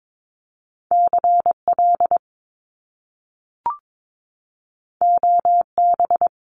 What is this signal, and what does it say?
Morse code, heard as a keyed tone, about 22 wpm, 700 Hz. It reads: CL OB
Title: Top Two Letter Combinations in Words 22wpm